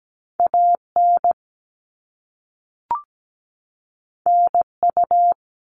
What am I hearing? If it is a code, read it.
Morse code, heard as a keyed tone, about 17 wpm, 700 Hz. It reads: AN NU
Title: All Cut Numbers - 2-Digit 17wpm